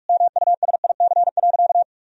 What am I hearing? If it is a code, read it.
Morse code, heard as a keyed tone, about 46 wpm, 700 Hz. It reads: MUSIC.
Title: Sentences from Top 1000 Words - Review Part 3 - 45wpm